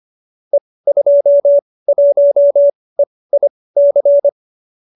Morse code, heard as a keyed tone, about 25 wpm, spaced as for 18 wpm, 570 Hz. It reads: E21EIC
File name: qso208b 18